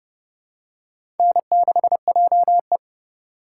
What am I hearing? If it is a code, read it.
Morse code, heard as a keyed tone, about 30 wpm, 700 Hz. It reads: N6JE
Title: USA Call Signs - 4 Characters - 3x Character Spacing - 30wpm